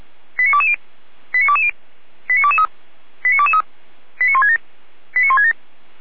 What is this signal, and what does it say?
Five-tone selective call (ZVEI), heard as quick runs of five tones, tones of about 70 ms. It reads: 89220 89220 89223 89223 89167 89167